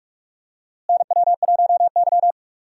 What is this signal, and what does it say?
Morse code, heard as a keyed tone, about 46 wpm, 700 Hz. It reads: NW1Y
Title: USA Call Signs - 4 Characters - Repeated 2x - 45wpm